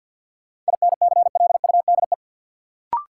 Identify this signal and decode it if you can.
Morse code, heard as a keyed tone, about 50 wpm, 700 Hz. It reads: INCLUDE
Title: Sentences from Top 1000 Words - Review Part 3 - New Word Drill - 50wpm